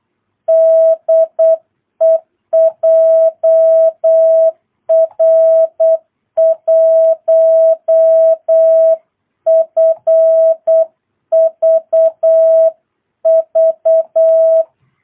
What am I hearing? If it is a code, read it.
Morse code, heard as a keyed tone, about 8 wpm, 650 Hz. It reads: DE JR1FVV
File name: jh8cbh / 2022cw